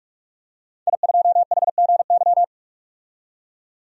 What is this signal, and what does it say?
Morse code, heard as a keyed tone, about 48 wpm, 700 Hz. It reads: I2HGY
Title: DXCC-100 Call Signs - 5 Characters - 45wpm